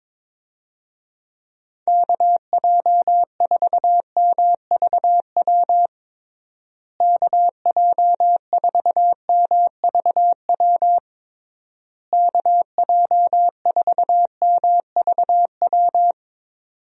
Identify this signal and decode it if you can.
Morse code, heard as a keyed tone, about 22 wpm, 700 Hz. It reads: KJ4MVW KJ4MVW KJ4MVW
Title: USA Call Signs - 6 Characters - Repeated 3x - 22wpm